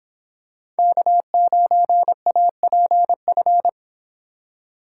Morse code, heard as a keyed tone, about 26 wpm, 700 Hz. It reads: K9APF
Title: Speed Racing - USA 1x3 Call Signs 22wpm